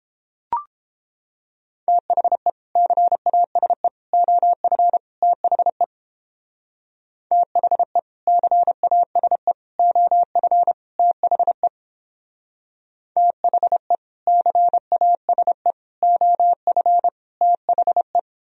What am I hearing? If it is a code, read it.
Morse code, heard as a keyed tone, about 31 wpm, 700 Hz. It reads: THE CASE OF THE THE CASE OF THE THE CASE OF THE